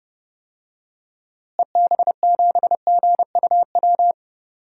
Morse code, heard as a keyed tone, about 30 wpm, 700 Hz. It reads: EB7GUW
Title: DXCC-100 Call Signs - 6 Characters - Repeated 2x - 30wpm